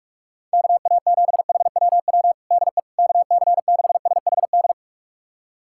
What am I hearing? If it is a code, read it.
Morse code, heard as a keyed tone, about 46 wpm, 700 Hz. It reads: KA7HWW DE KC6SHD
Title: USA Callsign DE Callsign 45wpm